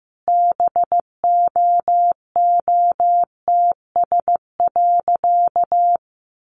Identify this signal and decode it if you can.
Morse code, heard as a keyed tone, about 15 wpm, 700 Hz. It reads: BOOTS.
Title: A Sound Of Thunder 15wpm